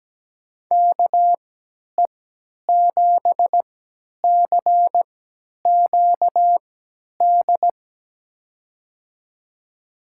Morse code, heard as a keyed tone, about 17 wpm, spaced as for 10 wpm, 700 Hz. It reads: KE7CQD
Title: USA Call Signs - 6 Characters - 3x Character Spacing - 17wpm